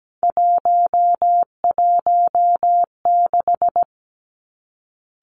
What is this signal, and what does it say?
Morse code, heard as a keyed tone, about 17 wpm, 700 Hz. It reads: 116
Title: Three Challenging Letter-Number Combinations 17wpm